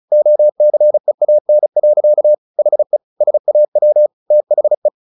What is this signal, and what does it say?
Morse code, heard as a keyed tone, about 34 wpm, 600 Hz. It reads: OCEAN. HE SAW THE
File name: the old man and the sea / the old man and the sea 1 35wpm